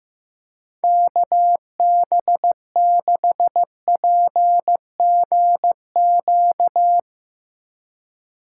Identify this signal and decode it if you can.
Morse code, heard as a keyed tone, about 15 wpm, 700 Hz. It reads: KB6PGQ
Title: USA Call Signs - 3x Character Spacing - 15wpm